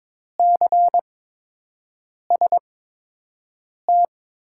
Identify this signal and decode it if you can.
Morse code, heard as a keyed tone, about 22 wpm, spaced as for 6 wpm, 700 Hz. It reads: CST
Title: USA Call Signs - 6 Characters - 8x Character Spacing - 22wpm